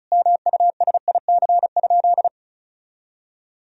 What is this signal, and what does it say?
Morse code, heard as a keyed tone, about 34 wpm, 700 Hz. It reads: MUSIC?
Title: Sentences from Top 300 Words and Abbreviations 35wpm